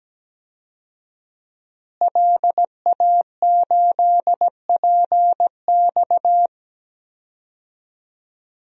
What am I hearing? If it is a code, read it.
Morse code, heard as a keyed tone, about 17 wpm, 700 Hz. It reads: LA8PX
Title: DXCC-100 Call Signs - 5 Characters - Repeated 3x - 17wpm